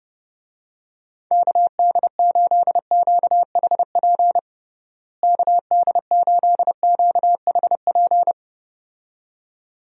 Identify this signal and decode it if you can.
Morse code, heard as a keyed tone, about 32 wpm, 700 Hz. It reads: KD8QHP KD8QHP
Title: USA Call Signs - 6 Characters - Repeated 2x - 30wpm